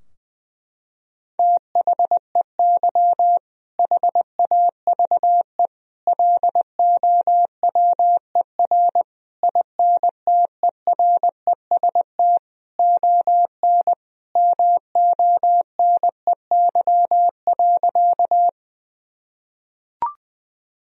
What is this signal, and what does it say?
Morse code, heard as a keyed tone, about 20 wpm, 700 Hz. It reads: THEY HAVE LOWER INTEREST ON MONEY.
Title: Sentences from Top 1000 Words - Review Part 4 - 20wpm